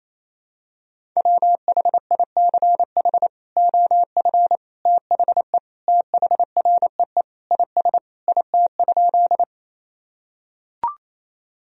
Morse code, heard as a keyed tone, about 28 wpm, 700 Hz. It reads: WHICH OF THE THREE IS IT?
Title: Sentences from Top 1000 Words - Review Part 1 - 28wpm